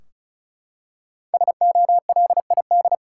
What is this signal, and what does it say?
Morse code, heard as a keyed tone, about 36 wpm, 700 Hz. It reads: SOLID
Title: Five Letter Words - 35wpm